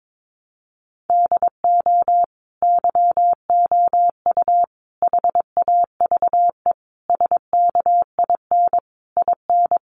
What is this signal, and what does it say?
Morse code, heard as a keyed tone, about 22 wpm, 700 Hz. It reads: DO YOU HAVE SKIN IN
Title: Sentences from Top 1300 Words 22wpm